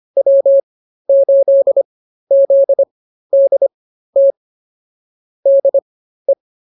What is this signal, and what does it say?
Morse code, heard as a keyed tone, about 24 wpm, spaced as for 13 wpm, 550 Hz. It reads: W8ZDT DE
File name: QSO 108 13